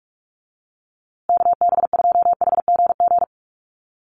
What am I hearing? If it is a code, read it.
Morse code, heard as a keyed tone, about 46 wpm, 700 Hz. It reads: KB2HGG
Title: USA Call Signs Repeated 2x 45wpm